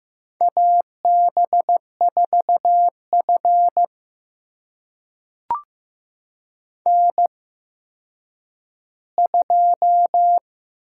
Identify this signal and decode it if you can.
Morse code, heard as a keyed tone, about 15 wpm, spaced as for 9 wpm, 700 Hz. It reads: AB4F N2
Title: USA Call Signs - 4 Characters - 8x Character Spacing - 15wpm